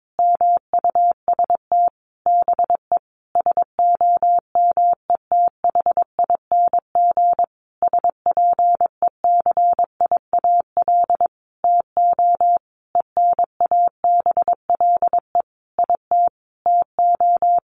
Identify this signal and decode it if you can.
Morse code, heard as a keyed tone, about 22 wpm, 700 Hz. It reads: MUST BE SOMETHING SPECIAL TO ENABLE IT TO
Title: Sentences from Top 1500 Words 22wpm